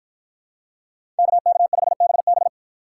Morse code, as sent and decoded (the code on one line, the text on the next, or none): -.- -.- ..... -... -...
KK5BB